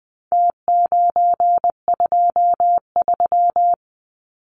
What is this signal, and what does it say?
Morse code, heard as a keyed tone, about 19 wpm, 700 Hz. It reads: T923
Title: Numbers - Contest Style - 4-Digit 20wpm